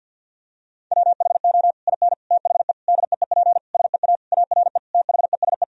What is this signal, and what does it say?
Morse code, heard as a keyed tone, about 50 wpm, 700 Hz. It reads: WHO IN THE DEEP SEA ARE THESE
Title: Sentences from Top 1000 Words - Review Part 3 - 50wpm